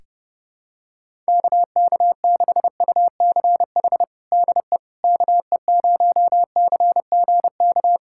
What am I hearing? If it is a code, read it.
Morse code, heard as a keyed tone, about 30 wpm, 700 Hz. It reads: KK6UCH DE KE0CGK
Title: USA Callsign DE Callsign 30wpm